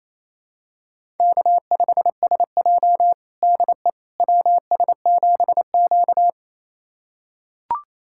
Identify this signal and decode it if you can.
Morse code, heard as a keyed tone, about 28 wpm, 700 Hz. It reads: K5SJ DE WS7Q
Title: USA Callsign DE Callsign - 4 Characters - 28wpm